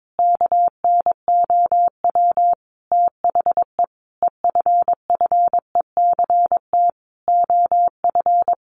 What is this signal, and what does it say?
Morse code, heard as a keyed tone, about 22 wpm, 700 Hz. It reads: KNOW THE EFFECT OF
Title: Sentences from Top 1400 Words 22wpm